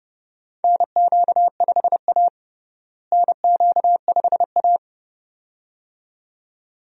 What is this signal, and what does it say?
Morse code, heard as a keyed tone, about 30 wpm, 700 Hz. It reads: NQ5A NQ5A
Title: USA Call Signs - 4 Characters - Repeated 2x - 30wpm